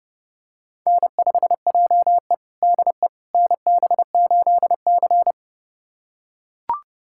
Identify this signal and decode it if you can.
Morse code, heard as a keyed tone, about 30 wpm, 700 Hz. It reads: N5JE DE NB8C
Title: USA Callsign DE Callsign - 4 Characters - 30wpm